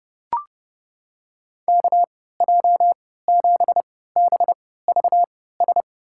Morse code, heard as a keyed tone, about 30 wpm, spaced as for 17 wpm, 700 Hz. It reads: KJ7BVS
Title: USA Call Signs - 3x Character Spacing - 30wpm